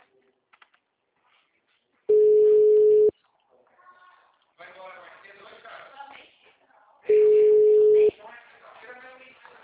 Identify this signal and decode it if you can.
Ringback tone (Europe, 425 Hz), with speech.